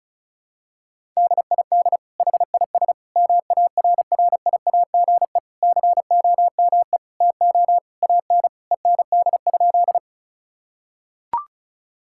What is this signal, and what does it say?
Morse code, heard as a keyed tone, about 36 wpm, 700 Hz. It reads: DID HIS MARRIAGE COME TO AN END?
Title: Sentences from Top 1000 Words - Review Part 5 - 35wpm